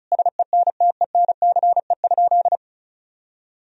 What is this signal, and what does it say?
Morse code, heard as a keyed tone, about 34 wpm, 700 Hz. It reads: SENTENCE?
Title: Sentences from Top 1000 Words - Review Part 1 - 35wpm